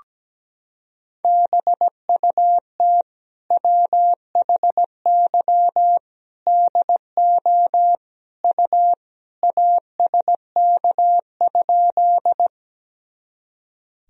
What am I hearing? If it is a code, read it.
Morse code, heard as a keyed tone, about 17 wpm, 700 Hz. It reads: BUT WHY DO U ASK?
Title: Sentences from Top 200 Words and Abbreviations 17wpm